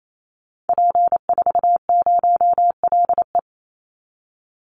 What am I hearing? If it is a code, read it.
Morse code, heard as a keyed tone, about 28 wpm, 700 Hz. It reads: P40LE
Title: Call Signs - 28wpm